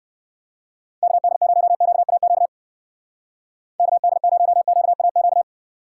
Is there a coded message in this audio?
Yes: Morse code from 1.0 to 5.4 s, heard as a keyed tone, about 57 wpm, 700 Hz. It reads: KD0QAQ KD0QAQ